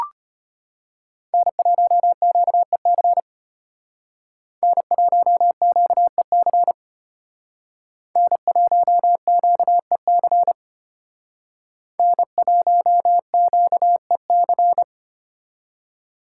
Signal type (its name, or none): Morse code